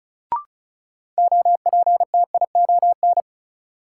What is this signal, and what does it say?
Morse code, heard as a keyed tone, about 36 wpm, 700 Hz. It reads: OPTION
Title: Sentences from Top 1300 Words - New Word Drill Alt - 35wpm